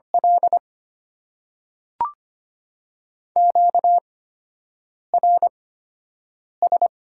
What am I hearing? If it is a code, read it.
Morse code, heard as a keyed tone, about 25 wpm, spaced as for 7 wpm, 700 Hz. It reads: L QRS
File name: Top 90 QSO Elements - 8x Character Spacing - 25wpm